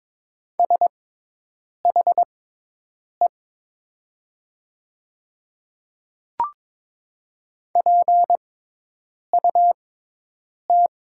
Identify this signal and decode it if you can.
Morse code, heard as a keyed tone, about 22 wpm, spaced as for 8 wpm, 700 Hz. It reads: SHE PUT
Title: 6x Character Spacing - Top 300 Words - 22wpm